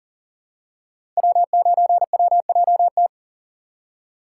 Morse code, heard as a keyed tone, about 42 wpm, 700 Hz. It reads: W9WJT
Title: Code Groups - 5 Letter:Numbers - 40wpm